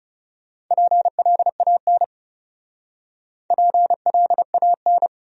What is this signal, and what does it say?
Morse code, heard as a keyed tone, about 37 wpm, 700 Hz. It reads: PLAN PLAN
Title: Speed-Racing Top 500 Words 30wpm